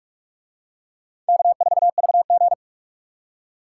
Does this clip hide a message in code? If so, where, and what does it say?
Yes: Morse code from 1.3 to 2.5 s, heard as a keyed tone, about 42 wpm, 700 Hz. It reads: K4VG